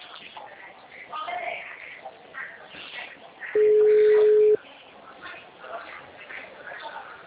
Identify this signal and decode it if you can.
Ringback tone (Europe, 425 Hz), with speech.
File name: out- -2015- - - .510